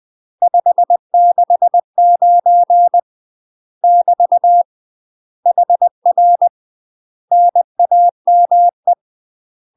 Morse code, heard as a keyed tone, about 20 wpm, 700 Hz. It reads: 569 <BT> HR NAME